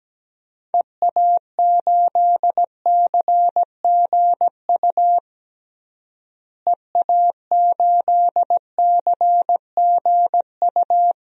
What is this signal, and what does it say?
Morse code, heard as a keyed tone, about 17 wpm, 700 Hz. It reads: EA8CGU EA8CGU
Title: DXCC-100 Call Signs - 6 Characters - Repeated 2x - 17wpm